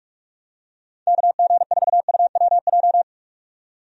Morse code, heard as a keyed tone, about 45 wpm, 700 Hz. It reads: KG4UWJ
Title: USA Call Signs - 6 Characters - 45wpm